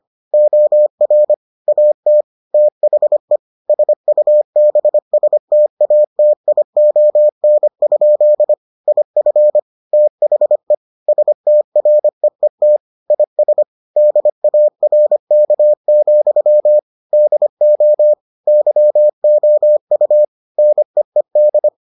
Morse code, heard as a keyed tone, about 25 wpm, 600 Hz. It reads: OR AT THE SUBSTATION? IF THE STREET IS DARK, DO YOU NEED